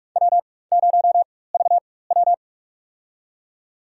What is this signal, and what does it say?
Morse code, heard as a keyed tone, about 45 wpm, spaced as for 21 wpm, 700 Hz. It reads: W0VW
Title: USA Call Signs - 4x Character Spacing - 45wpm